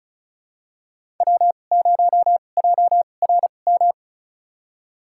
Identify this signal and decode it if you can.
Morse code, heard as a keyed tone, about 36 wpm, spaced as for 25 wpm, 700 Hz. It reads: W0JRM
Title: USA Call Signs - 5 Characters - 2x Character Spacing - 35wpm